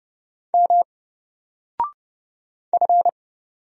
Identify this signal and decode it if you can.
Morse code, heard as a keyed tone, about 30 wpm, 700 Hz. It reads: M F
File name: Single-Letter-Number-30wpm